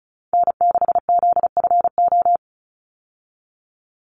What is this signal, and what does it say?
Morse code, heard as a keyed tone, about 34 wpm, 700 Hz. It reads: N6ZFO
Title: Call Signs - 35wpm